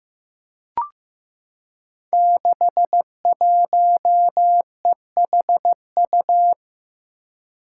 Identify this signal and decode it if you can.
Morse code, heard as a keyed tone, about 15 wpm, 700 Hz. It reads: 61EHU
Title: Five Challenging Letter-Number Combinations 15wpm